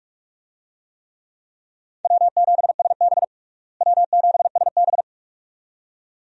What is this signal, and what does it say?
Morse code, heard as a keyed tone, about 42 wpm, 700 Hz. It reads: W7SB W7SB
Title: USA Call Signs - 4 Characters - Repeated 2x - 45wpm